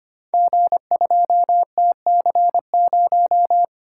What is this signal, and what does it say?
Morse code, heard as a keyed tone, about 25 wpm, 700 Hz. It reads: G2TC0
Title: Code Groups - 5 Letter:Numbers - 25wpm